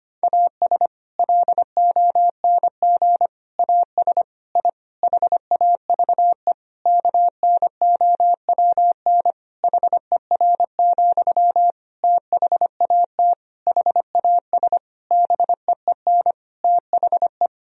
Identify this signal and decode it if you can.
Morse code, heard as a keyed tone, about 25 wpm, 700 Hz. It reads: AS LONG AS I HAVE KNOWN HER, THAT HAS BEEN THE
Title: Sentences from Top 1000 Words - Review Part 3 - 25wpm